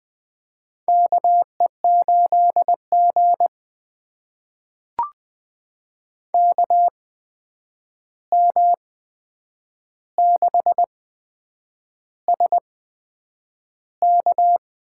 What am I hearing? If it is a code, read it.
Morse code, heard as a keyed tone, about 20 wpm, spaced as for 8 wpm, 700 Hz. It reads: KE8G KM6SK
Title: USA Call Signs - 8x Character Spacing - 20wpm